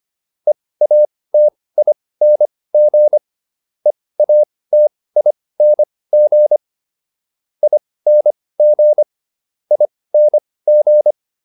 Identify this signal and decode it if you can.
Morse code, heard as a keyed tone, about 25 wpm, spaced as for 18 wpm, 600 Hz. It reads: EATING EATING ING ING